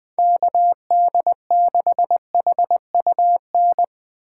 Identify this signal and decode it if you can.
Morse code, heard as a keyed tone, about 21 wpm, 700 Hz. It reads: KD6HUN
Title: USA Call Signs - 6 Characters - 2x Character Spacing - 20wpm